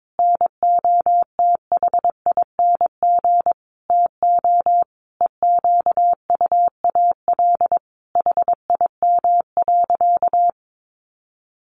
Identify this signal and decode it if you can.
Morse code, heard as a keyed tone, about 22 wpm, 700 Hz. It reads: NOTHING TO EQUAL HIM.
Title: Sentences from Top 1400 Words 22wpm